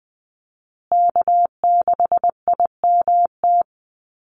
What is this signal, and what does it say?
Morse code, heard as a keyed tone, about 20 wpm, 700 Hz. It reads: K6IMT
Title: USA Call Signs Repeated 3x 20wpm